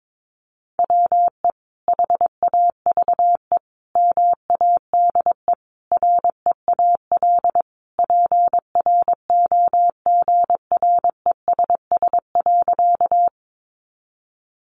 Morse code, heard as a keyed tone, about 22 wpm, 700 Hz. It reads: WE HAVE MADE REAL PROGRESS.
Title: Sentences from Top 1400 Words 22wpm